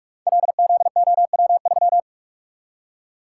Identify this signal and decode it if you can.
Morse code, heard as a keyed tone, about 44 wpm, 700 Hz. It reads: LZOW3
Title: Code Groups - 5 Letter:Numbers - 45wpm